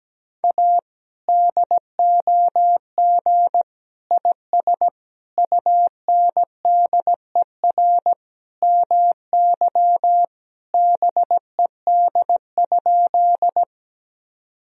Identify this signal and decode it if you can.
Morse code, heard as a keyed tone, about 17 wpm, 700 Hz. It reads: A DOG IS UNDER MY BED?
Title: Sentences from Top 1000 Words - Review Part 2 - 17wpm